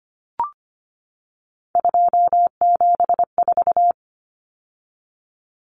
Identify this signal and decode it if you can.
Morse code, heard as a keyed tone, about 25 wpm, 700 Hz. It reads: 274
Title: Numbers - 3-Digit 25wpm